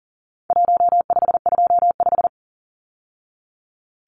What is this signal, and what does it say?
Morse code, heard as a keyed tone, about 40 wpm, 700 Hz. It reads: 1525
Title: Numbers - Contest Style - 4-Digit 40wpm